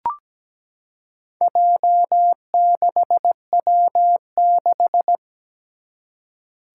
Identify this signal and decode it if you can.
Morse code, heard as a keyed tone, about 17 wpm, 700 Hz. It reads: J6W6
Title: Four Challenging Letter-Number Combinations 17wpm